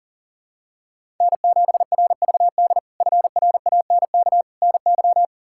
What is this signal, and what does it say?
Morse code, heard as a keyed tone, about 40 wpm, 700 Hz. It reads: N7RVD FRANK NY
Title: K1USN SST QSO Practice - 40wpm